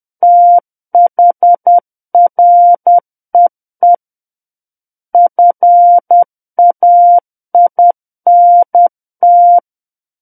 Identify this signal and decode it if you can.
Morse code, heard as a keyed tone, about 10 wpm, 700 Hz. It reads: THREE FAINT
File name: Book One / 01-Chapter 1 - 10 WPM